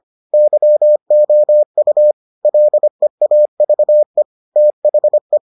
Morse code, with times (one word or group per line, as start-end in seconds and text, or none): 0.3-2.1 YOU
2.4-4.2 LEAVE
4.6-5.4 THE